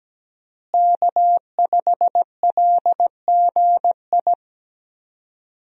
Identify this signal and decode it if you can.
Morse code, heard as a keyed tone, about 17 wpm, 700 Hz. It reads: K5LGI
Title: USA Call Signs - 5 Characters - Repeated 3x - 17wpm